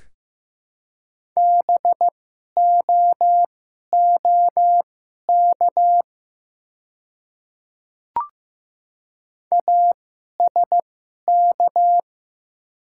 Morse code, heard as a keyed tone, about 15 wpm, spaced as for 11 wpm, 700 Hz. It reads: BOOK ASK
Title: 2x Character Spacing - Top 300 Words - 15wpm